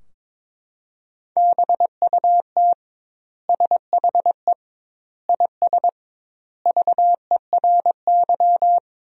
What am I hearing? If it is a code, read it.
Morse code, heard as a keyed tone, about 22 wpm, 700 Hz. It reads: BUT SHE IS VERY
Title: 2x Word-Spacing - Sentences from Top 400 Words - 22wpm